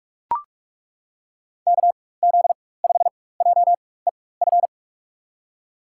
Morse code, heard as a keyed tone, about 41 wpm, spaced as for 20 wpm, 700 Hz. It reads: KZ5JEF
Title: USA Call Signs - 6 Characters - 4x Character Spacing - 45wpm